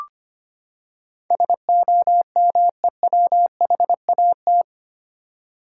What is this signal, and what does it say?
Morse code, heard as a keyed tone, about 25 wpm, 700 Hz. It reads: SOMEWHAT
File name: Eight Letter Words - 25wpm